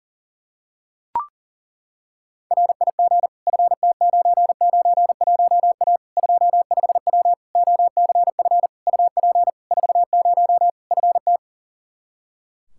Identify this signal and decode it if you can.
Morse code, heard as a keyed tone, about 40 wpm, 700 Hz. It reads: RIG FT991A 25W OCF UP 40 FT
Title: Practice QSO - Exchange 2 - Info - 40wpm